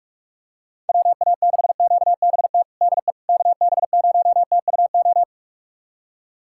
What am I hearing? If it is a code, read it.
Morse code, heard as a keyed tone, about 42 wpm, 700 Hz. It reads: WA6QBT DE KB0TUO
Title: USA Callsign DE Callsign 45wpm